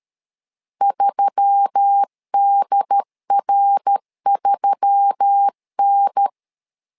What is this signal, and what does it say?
Morse code, heard as a keyed tone, about 13 wpm, 800 Hz. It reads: 3DR3N
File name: LEZIONI / 4 Q1S3 12wpm 2023